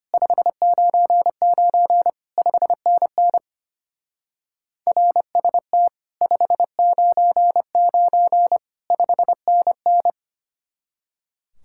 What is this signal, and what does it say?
Morse code, heard as a keyed tone, about 26 wpm, 700 Hz. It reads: 599 5NN RST 599 5NN
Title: Speed-Racing Common QSO Phrases Encore - 25wpm